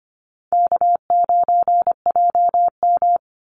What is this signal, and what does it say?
Morse code, heard as a keyed tone, about 25 wpm, 700 Hz. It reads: K9JM
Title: Call Signs Repeated 2x - 25wpm